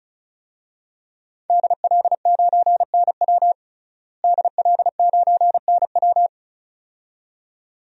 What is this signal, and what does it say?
Morse code, heard as a keyed tone, about 36 wpm, 700 Hz. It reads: DL9NW DL9NW
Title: DXCC-100 Call Signs - 5 Characters - Repeated 2x - 35wpm